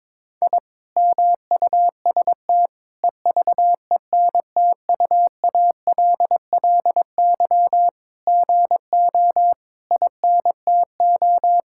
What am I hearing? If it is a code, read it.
Morse code, heard as a keyed tone, about 22 wpm, 700 Hz. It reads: I MUST EVENTUALLY GO INTO
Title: Sentences from Top 1000 Words - Review Part 6 - 22wpm